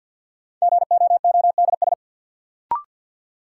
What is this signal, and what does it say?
Morse code, heard as a keyed tone, about 49 wpm, 700 Hz. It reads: GOODS
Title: Sentences from Top 1000 Words - Review Part 6 - New Word Drill - 50wpm